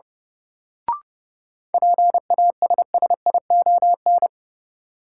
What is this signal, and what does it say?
Morse code, heard as a keyed tone, about 30 wpm, 700 Hz. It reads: PASSION
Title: Sentences from Top 1300 Words - New Word Drill Alt - 30wpm